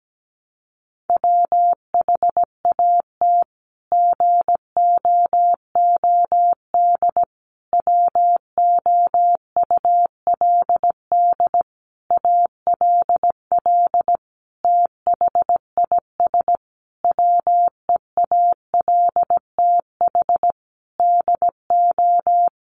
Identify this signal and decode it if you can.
Morse code, heard as a keyed tone, about 17 wpm, 700 Hz. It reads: WHAT GOOD WOULD ALL THIS WEALTH DO
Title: Sentences from Top 1200 Words 17wpm